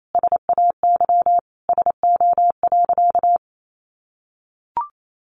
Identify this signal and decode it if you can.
Morse code, heard as a keyed tone, about 28 wpm, 700 Hz. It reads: SAY SO.
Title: Sentences from Top 1400 Words 28wpm